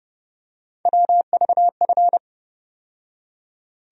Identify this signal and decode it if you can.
Morse code, heard as a keyed tone, about 30 wpm, 700 Hz. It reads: WVF
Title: USA Call Sign Postfixes - 3 Letters 30wpm